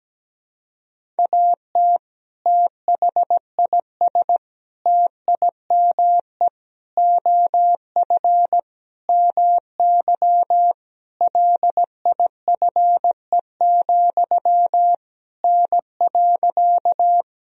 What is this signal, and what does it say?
Morse code, heard as a keyed tone, about 17 wpm, 700 Hz. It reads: AT THIS TIME OF MY LIFE, N.
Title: Sentences from Top 300 Words and Abbreviations 17wpm